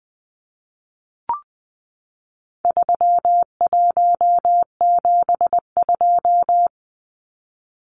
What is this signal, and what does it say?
Morse code, heard as a keyed tone, about 20 wpm, 700 Hz. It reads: 3172
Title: Numbers - 4-Digit 20wpm